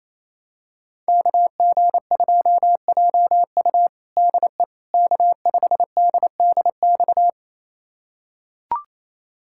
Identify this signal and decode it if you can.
Morse code, heard as a keyed tone, about 27 wpm, 690 Hz. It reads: KG2JU DE K5DDX
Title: USA Callsign DE Callsign - 5 Characters - 28wpm